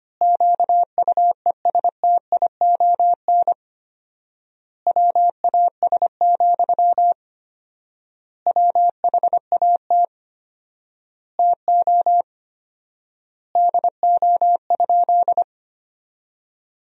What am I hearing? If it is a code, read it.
Morse code, heard as a keyed tone, about 25 wpm, 700 Hz. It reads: QUESTION WAS, WHAT TO DO?
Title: 4x Word-Spacing - Sentences from Top 400 Words - 25wpm